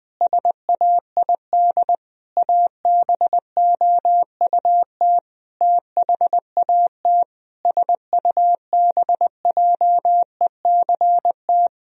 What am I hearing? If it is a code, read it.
Morse code, heard as a keyed tone, about 21 wpm, 700 Hz. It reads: SAID ABOUT THAT SUBJECT
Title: Sentences from Top 1000 Words - Review Part 2 - 20wpm